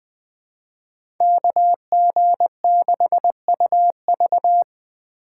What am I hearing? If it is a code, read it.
Morse code, heard as a keyed tone, about 20 wpm, 700 Hz. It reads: KG6UV
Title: USA Call Signs - 5 Characters - 4x Character Spacing - 20wpm